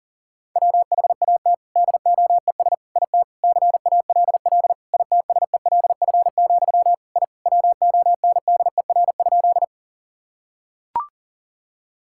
Morse code, heard as a keyed tone, about 40 wpm, 700 Hz. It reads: WHAT DOES IT CALL ITSELF, I WONDER?
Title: Sentences from Top 1000 Words - Review Part 5 - 40wpm